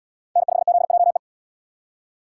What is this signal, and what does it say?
Morse code, heard as a keyed tone, about 68 wpm, 700 Hz. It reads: N5ZJE